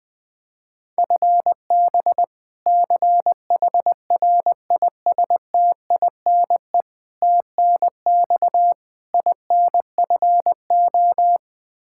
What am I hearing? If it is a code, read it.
Morse code, heard as a keyed tone, about 21 wpm, 700 Hz. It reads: FB CHRISTINE TNX INFO
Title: Practice QSO - Exchange 3 - More Info - 20wpm